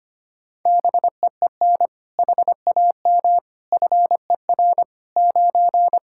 Morse code, heard as a keyed tone, about 25 wpm, 700 Hz. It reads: BEEN HAM FER 9
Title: Practice QSO - Exchange 3 - More Info - 25wpm